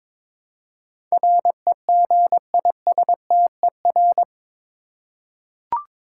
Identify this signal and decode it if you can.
Morse code, heard as a keyed tone, about 22 wpm, 700 Hz. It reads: REGISTER
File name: practice / 22wpm / Eight Letter Words - 22wpm